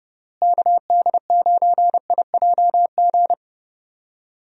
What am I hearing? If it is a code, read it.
Morse code, heard as a keyed tone, about 30 wpm, 700 Hz. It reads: KD9IJG
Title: USA Call Signs - 6 Characters - Repeated 3x - 30wpm